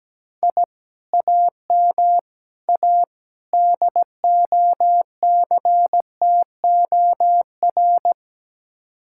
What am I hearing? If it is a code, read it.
Morse code, heard as a keyed tone, about 17 wpm, 700 Hz. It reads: I AM A DOCTOR
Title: Speed-Racing Common QSO Phrases - 17wpm